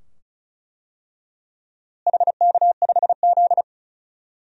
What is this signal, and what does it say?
Morse code, heard as a keyed tone, about 34 wpm, 700 Hz. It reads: HK5Z
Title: DXCC-100 Call Signs - 4 Characters - Repeated 2x - 35wpm